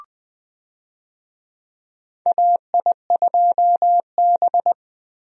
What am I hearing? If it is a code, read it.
Morse code, heard as a keyed tone, about 21 wpm, 700 Hz. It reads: AI2B
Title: USA Call Signs - 4 Characters - Repeated 2x - 20wpm